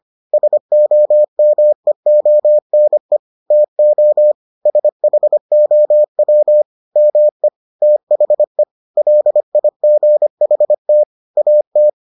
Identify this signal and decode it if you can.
Morse code, heard as a keyed tone, about 25 wpm, 600 Hz. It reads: SOMEONE TO SHOW ME THE LIGHT AT